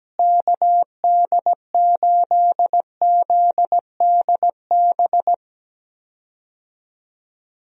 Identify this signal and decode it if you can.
Morse code, heard as a keyed tone, about 17 wpm, 700 Hz. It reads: KD8ZDB
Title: USA Call Signs - 6 Characters - Repeated 3x - 17wpm